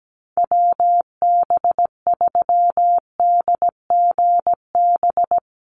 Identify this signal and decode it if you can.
Morse code, heard as a keyed tone, about 17 wpm, 700 Hz. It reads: WB3DGB
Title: USA Call Signs Repeated 3x 17wpm